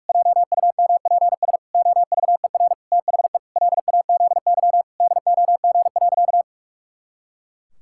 Morse code, heard as a keyed tone, about 46 wpm, 700 Hz. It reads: JUMPS OVER THE LAZY DOG.
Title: Pangram Challenge v2 - 45wpm